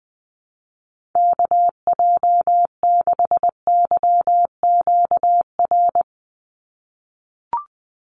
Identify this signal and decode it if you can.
Morse code, heard as a keyed tone, about 20 wpm, 700 Hz. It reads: KJ6YQR
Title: USA Call Signs 20wpm